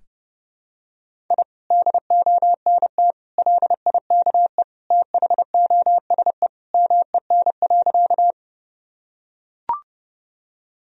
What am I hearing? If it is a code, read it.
Morse code, heard as a keyed tone, about 30 wpm, 700 Hz. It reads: I DONT LIKE THOSE MEN.
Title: Sentences from Top 1000 Words - Review Part 1 - 30wpm